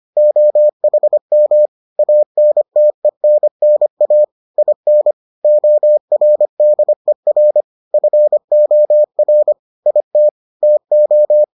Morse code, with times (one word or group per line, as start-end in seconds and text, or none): 0.2-1.7 OHM
2.0-4.2 ANTENNA
4.6-5.1 IN
5.4-7.6 ORDER
7.9-9.5 FOR
9.9-10.3 IT
10.6-11.4 TO